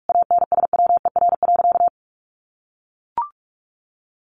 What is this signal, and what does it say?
Morse code, heard as a keyed tone, about 44 wpm, 700 Hz. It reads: ANSWER.
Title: Sentences from Top 1200 Words 45wpm